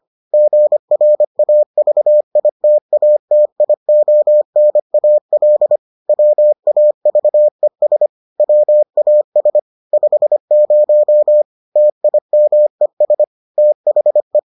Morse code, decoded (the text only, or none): GRAVITATIONAL WAVES WAS 50 TIMES THE